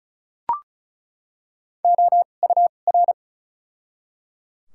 Morse code, heard as a keyed tone, about 34 wpm, spaced as for 25 wpm, 700 Hz. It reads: OUR
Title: 2x Character Spacing - Top 200 Words - 35wpm